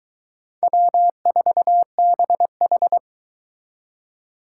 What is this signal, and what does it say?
Morse code, heard as a keyed tone, about 22 wpm, 700 Hz. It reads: W4BH